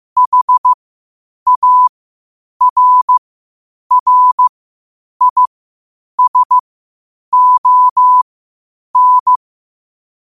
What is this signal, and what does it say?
Morse code, heard as a keyed tone, about 15 wpm, spaced as for 8 wpm, 1000 Hz. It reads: HARRISON
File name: qotdfiles / 8wpm